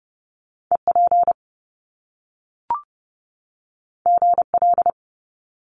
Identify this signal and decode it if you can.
Morse code, heard as a keyed tone, about 30 wpm, 700 Hz. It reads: EP GL
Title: Top Two Letter Combinations in Words 30wpm